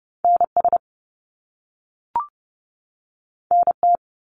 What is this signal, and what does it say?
Morse code, heard as a keyed tone, about 30 wpm, 700 Hz. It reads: NS NT
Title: Canadian Provinces and Territories 30wpm